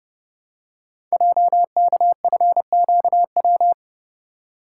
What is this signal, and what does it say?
Morse code, heard as a keyed tone, about 30 wpm, 700 Hz. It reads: JKFQW
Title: Code Groups - 5 Letters - 30wpm